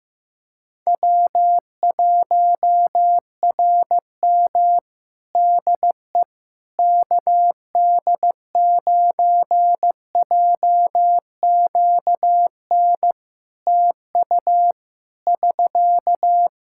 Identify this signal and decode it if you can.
Morse code, heard as a keyed tone, about 15 wpm, 700 Hz. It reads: W1RM DE KD9JQN TU <SK>
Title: K1USN SST QSO Practice 15wpm